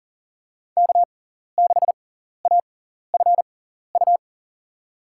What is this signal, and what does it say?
Morse code, heard as a keyed tone, about 40 wpm, spaced as for 14 wpm, 700 Hz. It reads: K6AFU
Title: USA Call Signs - 5 Characters - 6x Character Spacing - 40wpm